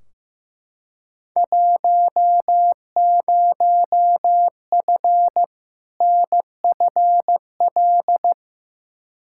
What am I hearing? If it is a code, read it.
Morse code, heard as a keyed tone, about 15 wpm, 700 Hz. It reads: 10F NFL
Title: Field Day Exchanges Encore - 15wpm